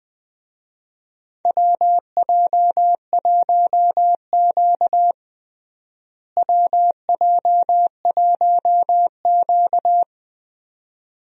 Morse code, heard as a keyed tone, about 20 wpm, 700 Hz. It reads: WJ1Q WJ1Q
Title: USA Call Signs - 4 Characters - Repeated 2x - 20wpm